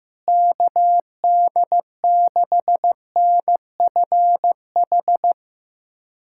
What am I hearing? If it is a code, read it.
Morse code, heard as a keyed tone, about 15 wpm, 700 Hz. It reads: KD6NFH
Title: USA Callsign DE Callsign - 6 Characters - 15wpm